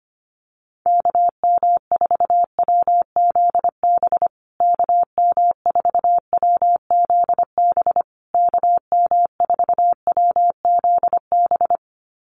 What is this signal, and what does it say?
Morse code, heard as a keyed tone, about 25 wpm, 700 Hz. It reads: KM4WZB KM4WZB KM4WZB
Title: USA Call Signs Repeated 3x 25wpm